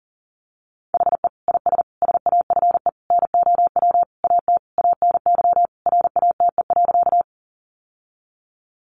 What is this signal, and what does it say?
Morse code, heard as a keyed tone, about 40 wpm, 700 Hz. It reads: HE IS SAFE NOW AT ANY RATE.
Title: Sentences from Top 900 Words 40wpm